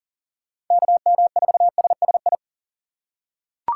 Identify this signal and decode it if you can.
Morse code, heard as a keyed tone, about 40 wpm, 700 Hz. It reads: KM4SSI
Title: USA Call Signs - 6 Characters - 3x Character Spacing - 40wpm